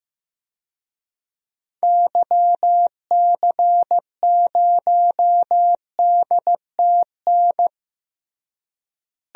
Morse code, heard as a keyed tone, about 15 wpm, 700 Hz. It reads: YC0DTN
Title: DXCC-100 Call Signs - 6 Characters - Repeated 2x - 15wpm